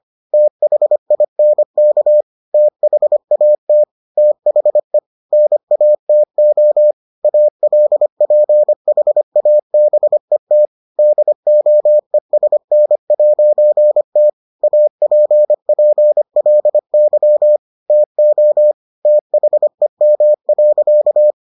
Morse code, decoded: THINK THAT THE NATO ALPHABET DOESN'T APPLY TO THEM.

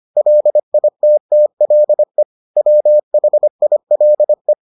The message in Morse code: LITTLE WHILE

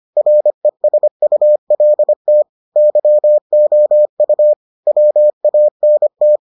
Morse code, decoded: RESULT YOU WANT

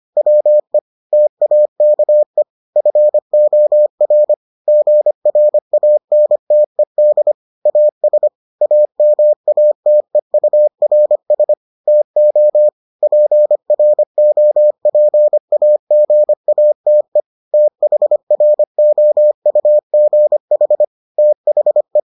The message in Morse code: WE TAKE FOR GRANTED AS AMATEURS TO PROPAGATE THROUGH THE

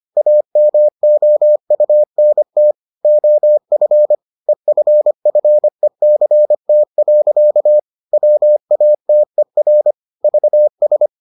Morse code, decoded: AMOUNT OF EFFECT. WATER VS

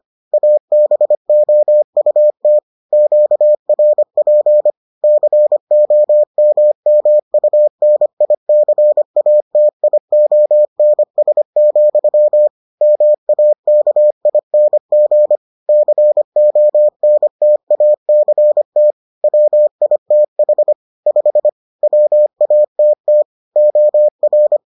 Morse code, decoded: ABOUT QRP COMMUNICATIONS, MAKING CONTACT WITH 5 WATT OR

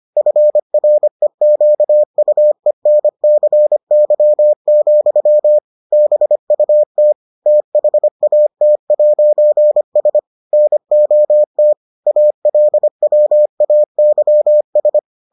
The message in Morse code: FREQUENCY, BUT THAT'S NOT ALWAYS

